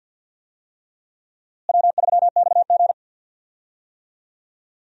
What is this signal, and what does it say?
Morse code, heard as a keyed tone, about 53 wpm, 700 Hz. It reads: W3XG
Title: USA Call Signs - 4 Characters - Repeated 3x - 50wpm